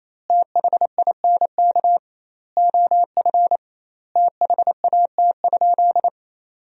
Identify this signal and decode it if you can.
Morse code, heard as a keyed tone, about 28 wpm, 700 Hz. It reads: THINK OF THAT?
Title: 2x Word-Spacing - Sentences from Top 200 Words - 28wpm